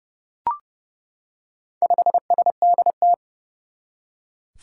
Morse code, heard as a keyed tone, about 30 wpm, 700 Hz. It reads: 5SDT
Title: Four Challenging Letter-Number Combinations 30wpm